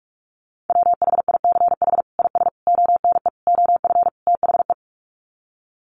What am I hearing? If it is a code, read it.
Morse code, heard as a keyed tone, about 44 wpm, 700 Hz. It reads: WHICH IS ONE OF THE
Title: Sets of 5 Words Encore 45wpm